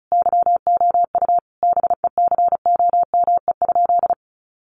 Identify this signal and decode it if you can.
Morse code, heard as a keyed tone, about 34 wpm, 700 Hz. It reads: YOU BECOME?
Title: Sentences from Top 600 Words 35wpm